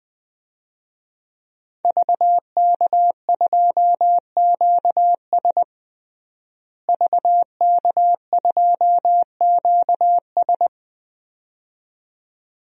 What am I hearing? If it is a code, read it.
Morse code, heard as a keyed tone, about 20 wpm, 700 Hz. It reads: VK2QS VK2QS
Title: DXCC-100 Call Signs - 5 Characters - Repeated 2x - 20wpm